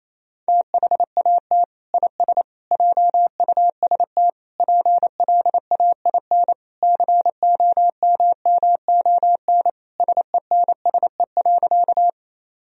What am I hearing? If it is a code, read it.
Morse code, heard as a keyed tone, about 27 wpm, 700 Hz. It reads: THAT IS JUST PLAIN COMMON SENSE.
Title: Sentences from Top 1000 Words - Review Part 4 - 28wpm